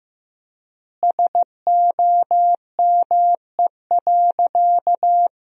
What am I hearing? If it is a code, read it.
Morse code, heard as a keyed tone, about 16 wpm, 700 Hz. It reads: SOME.
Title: 4x Word-Spacing - Sentences from Top 100 Words - 15wpm